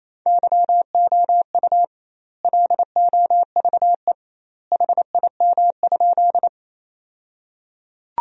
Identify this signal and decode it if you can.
Morse code, heard as a keyed tone, about 28 wpm, 700 Hz. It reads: YOU LOVE HIM?
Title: 2x Word-Spacing - Sentences from Top 400 Words - 28wpm